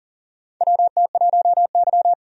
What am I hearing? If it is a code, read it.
Morse code, heard as a keyed tone, about 40 wpm, 700 Hz. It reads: WT1Y
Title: USA Call Signs - 4 Characters - 40wpm